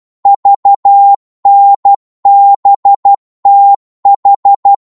Morse code, heard as a keyed tone, about 12 wpm, 800 Hz. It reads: VNBTH
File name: Morse / ltr-12wpm